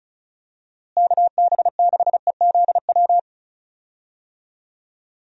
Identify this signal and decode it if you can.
Morse code, heard as a keyed tone, about 36 wpm, 700 Hz. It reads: KB6EZW
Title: USA Call Signs - 6 Characters - Repeated 2x - 35wpm